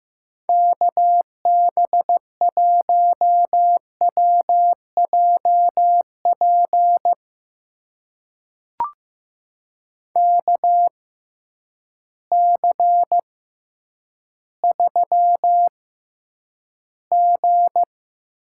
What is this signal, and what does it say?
Morse code, heard as a keyed tone, about 16 wpm, spaced as for 9 wpm, 700 Hz. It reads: KB1WJP KC3G
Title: USA Call Signs - 6 Characters - 6x Character Spacing - 15wpm